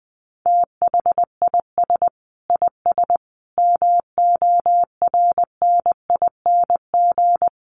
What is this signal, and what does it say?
Morse code, heard as a keyed tone, about 20 wpm, 700 Hz. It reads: THIS IS MORNING
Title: Sets of 5 Words Encore2 20wpm